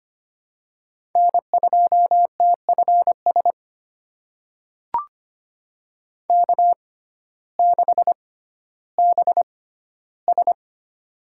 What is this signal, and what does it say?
Morse code, heard as a keyed tone, about 25 wpm, spaced as for 14 wpm, 700 Hz. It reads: N2TFS K6BS
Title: USA Call Signs - 6x Character Spacing - 25wpm